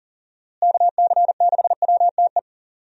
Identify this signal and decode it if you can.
Morse code, heard as a keyed tone, about 39 wpm, 700 Hz. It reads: KC6WTE
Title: USA Call Signs - 6 Characters - Repeated 2x - 40wpm